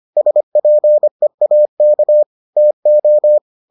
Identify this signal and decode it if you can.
Morse code, heard as a keyed tone, about 25 wpm, 600 Hz. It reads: SPEAK TO